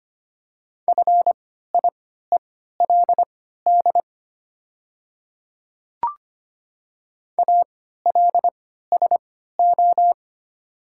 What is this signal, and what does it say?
Morse code, heard as a keyed tone, about 25 wpm, spaced as for 14 wpm, 700 Hz. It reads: FIELD ALSO
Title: 3x Character Spacing - Top 500 Words - 25wpm